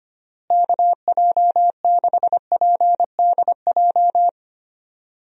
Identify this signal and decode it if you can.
Morse code, heard as a keyed tone, about 25 wpm, 700 Hz. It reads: KJ6PDJ
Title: USA Call Signs - 6 Characters - Repeated 2x - 25wpm